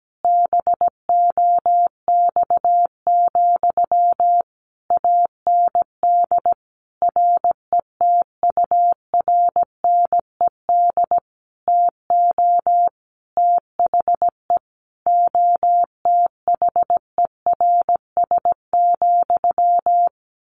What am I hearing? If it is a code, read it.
Morse code, heard as a keyed tone, about 17 wpm, 700 Hz. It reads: BOX, AND RETURNED TO THE OTHERS,
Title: A Sound Of Thunder 17wpm